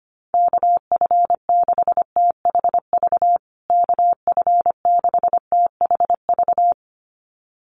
Morse code, heard as a keyed tone, about 25 wpm, 700 Hz. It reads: KF6THV KF6THV
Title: USA Call Signs Repeated 2x 25wpm